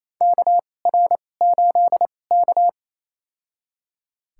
Morse code, heard as a keyed tone, about 28 wpm, spaced as for 20 wpm, 700 Hz. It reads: KR8K
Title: USA Call Signs - 4 Characters - 2x Character Spacing - 28wpm